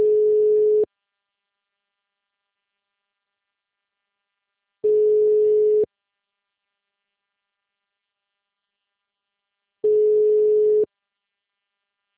Ringback tone (Europe, 425 Hz).